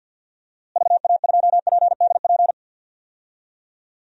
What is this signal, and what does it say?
Morse code, heard as a keyed tone, about 53 wpm, 700 Hz. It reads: VA2PDP
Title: DXCC-100 Call Signs - 6 Characters - Repeated 2x - 50wpm